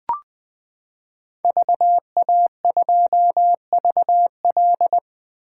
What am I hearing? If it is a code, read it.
Morse code, heard as a keyed tone, about 20 wpm, 700 Hz. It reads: VA2VL
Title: 2023 SOTA Chaser Call Signs - 20wpm